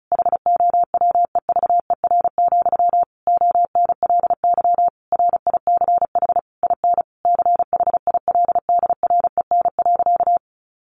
Morse code, heard as a keyed tone, about 34 wpm, 700 Hz. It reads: HOWEVER, ONLY RICH IN CHILDREN.
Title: Sentences from Top 1100 Words 35wpm